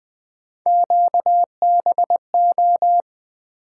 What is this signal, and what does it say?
Morse code, heard as a keyed tone, about 20 wpm, 700 Hz. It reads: QBO
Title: Three Letter-Number Combinations - 20wpm